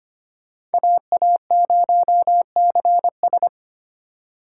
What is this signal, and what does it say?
Morse code, heard as a keyed tone, about 32 wpm, 700 Hz. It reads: AA0CS